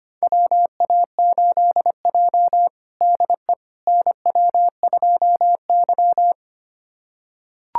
Morse code, heard as a keyed tone, about 25 wpm, 700 Hz. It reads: WA8J DE NW2Y
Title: USA Callsign DE Callsign - 4 Characters - 25wpm